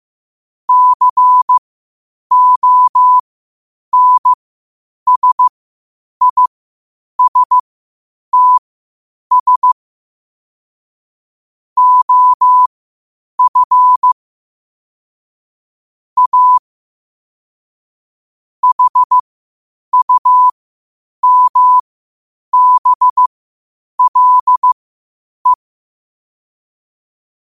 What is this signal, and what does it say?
Morse code, heard as a keyed tone, about 15 wpm, spaced as for 8 wpm, 1000 Hz. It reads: CONSISTS OF A HUMBLE
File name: qotdfiles / 8wpm